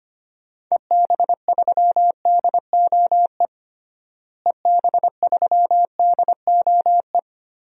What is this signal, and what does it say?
Morse code, heard as a keyed tone, about 25 wpm, 700 Hz. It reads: EB3DOE EB3DOE
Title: DXCC-100 Call Signs - 6 Characters - Repeated 3x - 25wpm